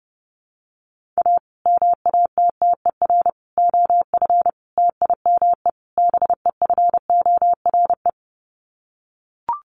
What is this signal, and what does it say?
Morse code, heard as a keyed tone, about 30 wpm, 700 Hz. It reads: A MATTER OF TIME BEFORE
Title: Sets of 5 Words Encore 30wpm